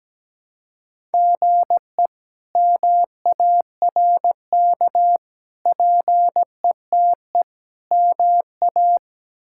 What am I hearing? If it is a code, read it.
Morse code, heard as a keyed tone, about 17 wpm, 700 Hz. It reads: GE MARK PETE MA
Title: K1USN SST QSO Practice - 17wpm